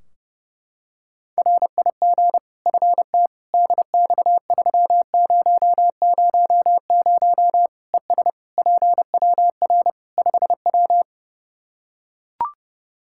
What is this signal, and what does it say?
Morse code, heard as a keyed tone, about 30 wpm, 700 Hz. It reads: RIG FT DX3000 ES PWR 5W
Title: Practice QSO - Exchange 2 - Info - 30wpm